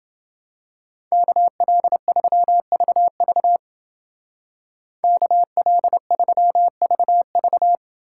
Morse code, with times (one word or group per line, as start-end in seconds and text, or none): 1.1-3.6 KL3VV
5.0-7.8 KL3VV